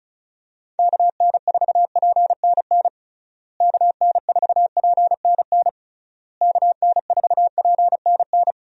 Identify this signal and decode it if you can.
Morse code, heard as a keyed tone, about 36 wpm, 700 Hz. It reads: KN4PNN KN4PNN KN4PNN
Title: USA Call Signs - 6 Characters - Repeated 3x - 35wpm